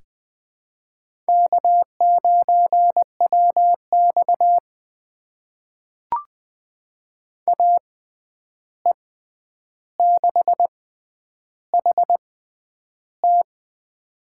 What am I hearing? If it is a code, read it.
Morse code, heard as a keyed tone, about 20 wpm, spaced as for 10 wpm, 700 Hz. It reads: K9WX AE6HT
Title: USA Call Signs - 6x Character Spacing - 20wpm